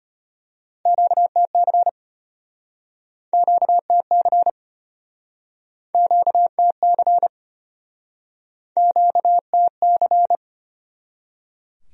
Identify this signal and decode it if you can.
Morse code, heard as a keyed tone, about 34 wpm, 700 Hz. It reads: QTC QTC QTC QTC